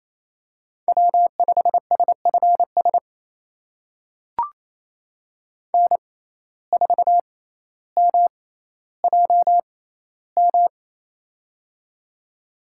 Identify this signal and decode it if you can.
Morse code, heard as a keyed tone, about 28 wpm, spaced as for 14 wpm, 700 Hz. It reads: W5SFS N4MJM
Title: USA Call Signs - 6x Character Spacing - 28wpm